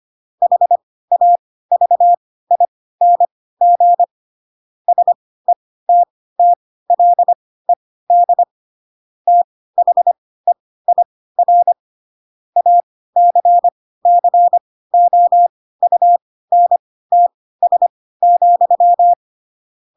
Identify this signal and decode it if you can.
Morse code, heard as a keyed tone, about 25 wpm, spaced as for 16 wpm, 700 Hz. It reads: HAVING SETTLED THEIR ACCOUNTS,
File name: Pinocchio / Pinocchio ch2 10 25 ik1hge